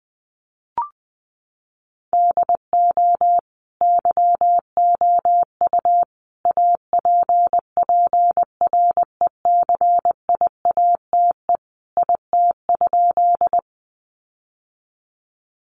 Morse code, heard as a keyed tone, about 20 wpm, 700 Hz. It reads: DO YOU APPRECIATE IT?
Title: Sentences from Top 1300 Words 20wpm